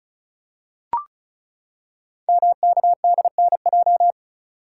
Morse code, heard as a keyed tone, about 36 wpm, 700 Hz. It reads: MKDNJ
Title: Code Groups - 5 Letters - 35wpm